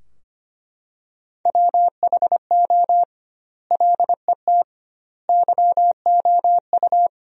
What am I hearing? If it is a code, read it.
Morse code, heard as a keyed tone, about 25 wpm, 700 Hz. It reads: WHO LET YOU
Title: 2x Word-Spacing - Sentences from Top 300 Words - 25wpm